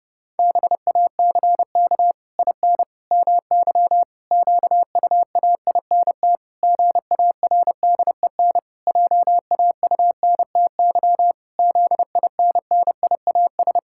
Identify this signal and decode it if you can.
Morse code, heard as a keyed tone, about 30 wpm, 700 Hz. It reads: BACK IN MY QUAINT GARDEN JAUNTY ZINNIAS
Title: Pangram Challenge v2 - 30wpm